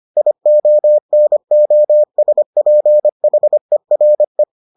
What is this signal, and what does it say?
Morse code, heard as a keyed tone, about 25 wpm, 600 Hz. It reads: IONOSPHERE